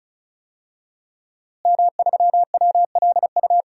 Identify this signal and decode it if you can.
Morse code, heard as a keyed tone, about 36 wpm, 700 Hz. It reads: M3WLU
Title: DXCC-100 Call Signs - 5 Characters - Repeated 3x - 35wpm